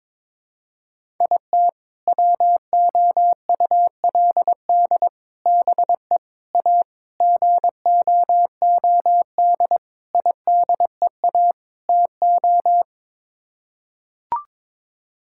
Morse code, heard as a keyed tone, about 22 wpm, 700 Hz. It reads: IT WOULD BE A GOOD IDEA TO
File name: practice / 22wpm / Sets of 7 Words - 22wpm